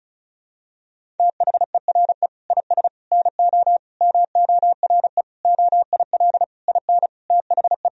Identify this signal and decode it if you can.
Morse code, heard as a keyed tone, about 36 wpm, 700 Hz. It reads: THERE IS NO MORE OIL IN THE
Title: Sentences from Top 1000 Words - Review Part 4 - 35wpm